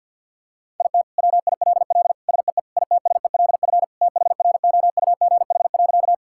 Morse code, heard as a keyed tone, about 50 wpm, 700 Hz. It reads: IT WILL SEE ITSELF THROUGH.
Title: Sentences from Top 1000 Words - Review Part 5 - 50wpm